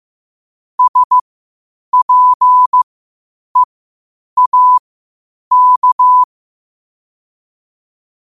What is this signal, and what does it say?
Morse code, heard as a keyed tone, about 15 wpm, spaced as for 8 wpm, 1000 Hz. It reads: SPEAK